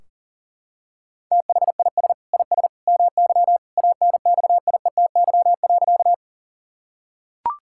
Morse code, heard as a keyed tone, about 40 wpm, 700 Hz. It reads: THIS IS MY ANXIETY.
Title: Sentences from Top 1600 Words 40wpm